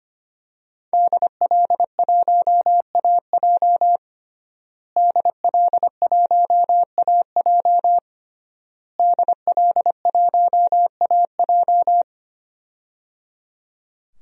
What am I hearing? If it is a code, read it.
Morse code, heard as a keyed tone, about 25 wpm, 700 Hz. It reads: DL1AJ DL1AJ DL1AJ
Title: DXCC-100 Call Signs - 5 Characters - Repeated 3x - 25wpm